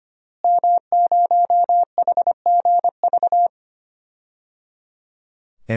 Morse code, heard as a keyed tone, about 25 wpm, 700 Hz. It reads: M0HGV
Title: DXCC-100 Call Signs - 5 Characters - Repeated 3x - 25wpm